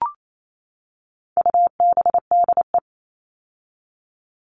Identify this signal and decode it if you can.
Morse code, heard as a keyed tone, about 28 wpm, 700 Hz. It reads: UBDE
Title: All Cut Numbers - 4-Digit 28wpm